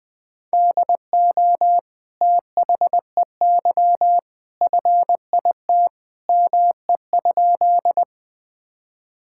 Morse code, heard as a keyed tone, about 20 wpm, 700 Hz. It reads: DO THEY FIT ME?
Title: Sentences from Top 1000 Words - Review Part 5 - 20wpm